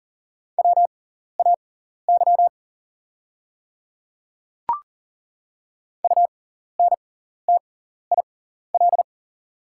Morse code, heard as a keyed tone, about 40 wpm, spaced as for 14 wpm, 700 Hz. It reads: WAY UNTIL
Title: 6x Character Spacing - Top 300 Words - 40wpm